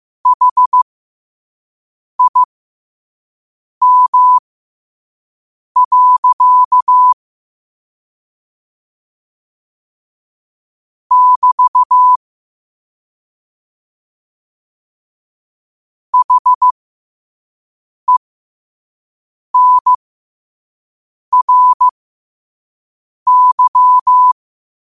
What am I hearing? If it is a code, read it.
Morse code, heard as a keyed tone, about 15 wpm, spaced as for 5 wpm, 1000 Hz. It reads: HIM. <BT> HENRY
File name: qotdfiles / 5wpm